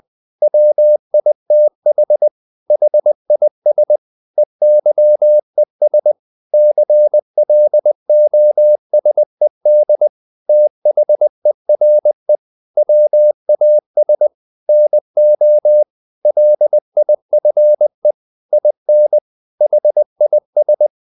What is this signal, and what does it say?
Morse code, heard as a keyed tone, about 20 wpm, 600 Hz. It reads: WITH HIS EYES CLOSED THERE WAS NO LIFE IN HIS
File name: the old man and the sea / the old man and the sea part 1 20wpm